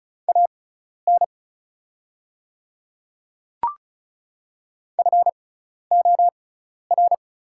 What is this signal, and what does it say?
Morse code, heard as a keyed tone, about 36 wpm, spaced as for 12 wpm, 700 Hz. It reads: AN FOR
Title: 6x Character Spacing - Top 100 Words - 35wpm